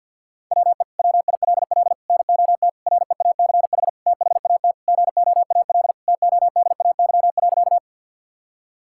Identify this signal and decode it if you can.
Morse code, heard as a keyed tone, about 47 wpm, 700 Hz. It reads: WE WILL NOT REACH THAT GOAL TODAY.
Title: Sentences from Top 1000 Words - Review Part 5 - 50wpm